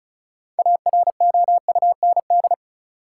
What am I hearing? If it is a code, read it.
Morse code, heard as a keyed tone, about 34 wpm, 700 Hz. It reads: AROUND
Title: Sentences from Top 600 Words - New Word Drill - 35wpm